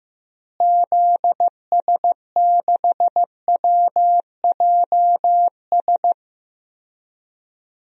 Morse code, heard as a keyed tone, about 15 wpm, 700 Hz. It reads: ZS6WJS
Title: DXCC-100 Call Signs - 6 Characters - Repeated 3x - 15wpm